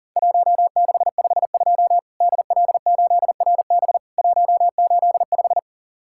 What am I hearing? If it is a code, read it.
Morse code, heard as a keyed tone, about 37 wpm, 700 Hz. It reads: 1652 DL8RB 185
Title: Worked All Europe DX Contest - QTC Exchange - 40wpm